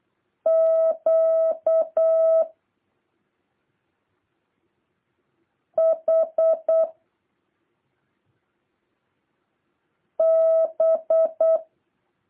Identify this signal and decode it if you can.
Morse code, heard as a keyed tone, about 8 wpm, 650 Hz. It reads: Q H B
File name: jh8cbh / 2023cw